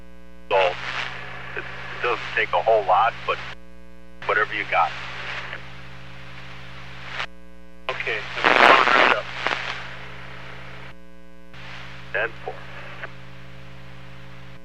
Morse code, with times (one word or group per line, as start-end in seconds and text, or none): none